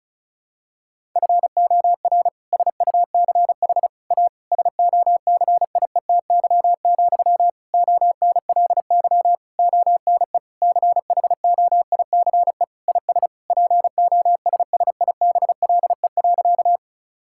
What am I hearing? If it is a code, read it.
Morse code, heard as a keyed tone, about 34 wpm, 700 Hz. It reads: FOR SUCH A SOCIETY, ONLY ONE CHOICE IS POSSIBLE.
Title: Sentences from Top 1000 Words - Review Part 6 - 35wpm